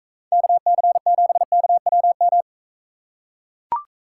Morse code, heard as a keyed tone, about 44 wpm, 700 Hz. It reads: KC7KWM